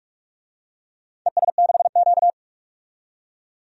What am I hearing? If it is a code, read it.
Morse code, heard as a keyed tone, about 47 wpm, 700 Hz. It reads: ES6Q
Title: DXCC-100 Call Signs - 4 Characters - 45wpm